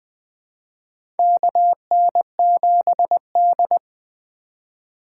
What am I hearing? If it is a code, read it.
Morse code, heard as a keyed tone, about 20 wpm, 700 Hz. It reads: KN7D
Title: 2023 POTA Hunter Call Signs - 20wpm